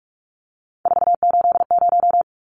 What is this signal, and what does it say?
Morse code, heard as a keyed tone, about 44 wpm, 700 Hz. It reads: 480
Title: Numbers - 3-Digit 45wpm